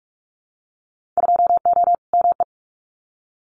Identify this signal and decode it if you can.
Morse code, heard as a keyed tone, about 44 wpm, 690 Hz. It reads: 2O ME
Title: Winter Field Day Exchanges Encore 45wpm